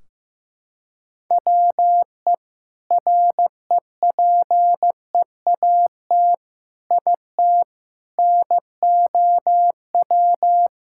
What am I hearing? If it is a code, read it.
Morse code, heard as a keyed tone, about 15 wpm, 700 Hz. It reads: WE REPEAT IT NOW
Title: Sentences from Top 1600 Words 15wpm